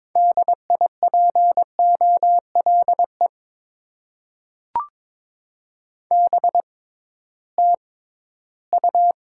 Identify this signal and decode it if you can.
Morse code, heard as a keyed tone, about 22 wpm, spaced as for 14 wpm, 700 Hz. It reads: DIPOLE BTU
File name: Top 90 QSO Elements - 6x Character Spacing - 22wpm